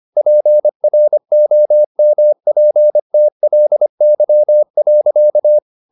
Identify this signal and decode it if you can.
Morse code, heard as a keyed tone, about 25 wpm, 600 Hz. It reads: PROMPTLY.
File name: the old man and the sea part 1 25wpm